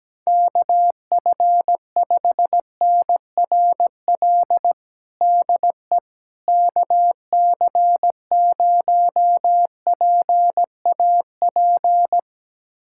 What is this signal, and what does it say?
Morse code, heard as a keyed tone, about 17 wpm, 700 Hz. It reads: KF5NRL DE KC0PAP
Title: USA Callsign DE Callsign 17wpm